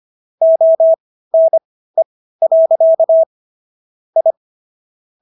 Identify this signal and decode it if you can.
Morse code, heard as a keyed tone, about 25 wpm, spaced as for 15 wpm, 650 Hz. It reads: ONE. I